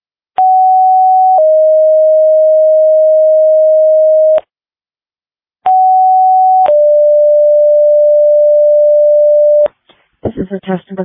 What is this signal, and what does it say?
Two-tone page, heard as a one-second tone, then a three-second tone, about 750 then 620 Hz, about 750 then 590 Hz, with speech. It is heard from 0.4 to 9.7 s.